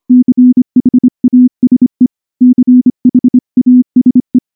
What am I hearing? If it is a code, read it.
Morse code, heard as a keyed tone, about 24 wpm, 250 Hz. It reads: CHASE CHASE
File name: morse / morse-training-25WPM